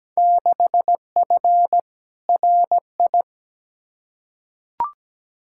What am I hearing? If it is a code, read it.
Morse code, heard as a keyed tone, about 17 wpm, 700 Hz. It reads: 6F RI
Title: Field Day Exchanges Encore - 17wpm